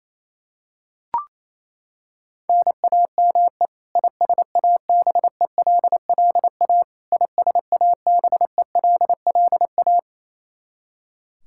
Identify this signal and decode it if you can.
Morse code, heard as a keyed tone, about 28 wpm, 700 Hz. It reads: NAME ISABELLA ISABELLA
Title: Practice QSO - Exchange 1 - Beginning - 28wpm